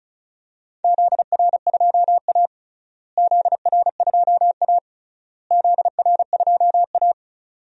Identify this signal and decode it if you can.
Morse code, heard as a keyed tone, about 34 wpm, 700 Hz. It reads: ZR2A ZR2A ZR2A
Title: DXCC-100 Call Signs - 4 Characters - Repeated 3x - 35wpm